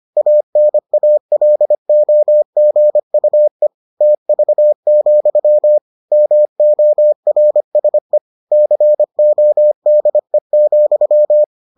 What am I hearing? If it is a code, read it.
Morse code, heard as a keyed tone, about 25 wpm, 600 Hz. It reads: ANALOGUE TV, MORSE CODE,